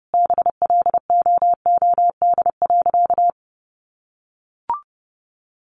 Morse code, heard as a keyed tone, about 30 wpm, 700 Hz. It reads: BLOOD.
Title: Sentences from Top 1200 Words 30wpm